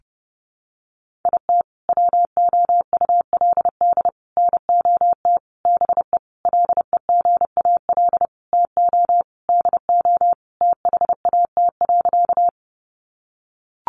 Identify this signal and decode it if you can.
Morse code, heard as a keyed tone, about 30 wpm, 700 Hz. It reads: IT WOULD NOT BE LEGAL TO DO THAT.
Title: Sentences from Top 1200 Words 30wpm